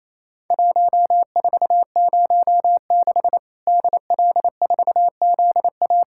Morse code, heard as a keyed tone, about 28 wpm, 700 Hz. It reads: 1406 DL4ZA
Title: Worked All Europe DX Contest - QTC Exchange - 28wpm